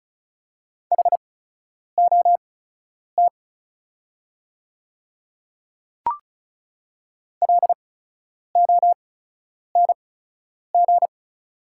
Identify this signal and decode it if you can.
Morse code, heard as a keyed tone, about 33 wpm, spaced as for 9 wpm, 690 Hz. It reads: HOT LONG
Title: 8x Character Spacing - Top 200 Words - 35wpm